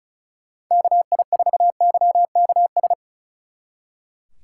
Morse code, heard as a keyed tone, about 45 wpm, 700 Hz. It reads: KI4YKS